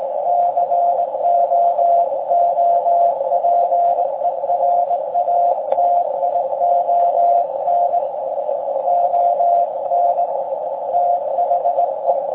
Morse code, heard as a keyed tone, about 19 wpm, 690 Hz. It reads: COOPERATION ON THE
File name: forum / im / aq grimeton